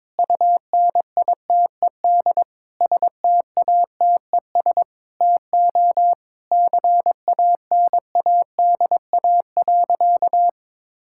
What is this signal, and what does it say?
Morse code, heard as a keyed tone, about 21 wpm, 700 Hz. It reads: UNITED STATES TO CANADA.
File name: practice / 22wpm / Sentences with 4-Word Phrases - 22wpm